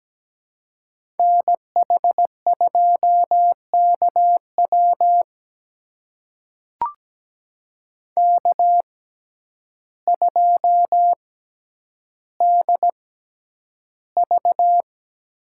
Morse code, heard as a keyed tone, about 17 wpm, spaced as for 9 wpm, 700 Hz. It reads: NH2KW K2DV
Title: USA Call Signs - 5 Characters - 6x Character Spacing - 17wpm